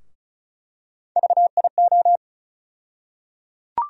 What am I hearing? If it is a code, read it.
Morse code, heard as a keyed tone, about 36 wpm, 700 Hz. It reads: VIO
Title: USA Call Sign Postfixes - 3 Letters 35wpm